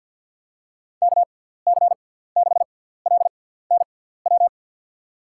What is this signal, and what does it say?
Morse code, heard as a keyed tone, about 52 wpm, spaced as for 17 wpm, 700 Hz. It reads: KC6LNW
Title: USA Call Signs - 6 Characters - 6x Character Spacing - 50wpm